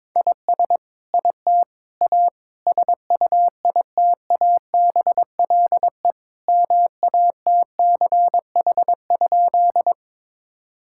Morse code, heard as a keyed tone, about 21 wpm, 700 Hz. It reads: IS IT A SUITABLE MATCH?
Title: Sentences from Top 1600 Words 22wpm